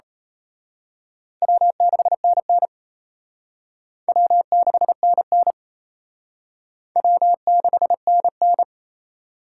Morse code, heard as a keyed tone, about 32 wpm, 700 Hz. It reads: W6NN W6NN W6NN
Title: Speed Racing - USA Call Signs 28wpm